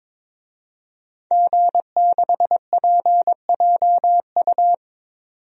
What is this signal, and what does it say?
Morse code, heard as a keyed tone, about 22 wpm, 700 Hz. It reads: G6PJU
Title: DXCC-100 Call Signs - 5 Characters - 22wpm